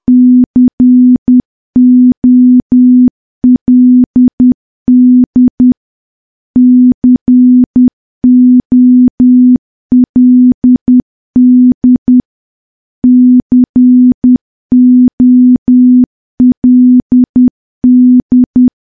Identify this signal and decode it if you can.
Morse code, heard as a keyed tone, about 10 wpm, 250 Hz. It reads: COLD COLD COLD
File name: morse / morse-training-10WPM